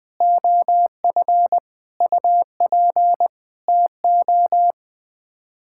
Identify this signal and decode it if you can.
Morse code, heard as a keyed tone, about 20 wpm, 700 Hz. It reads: OF UP TO
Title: Sets of 9 Words - 20wpm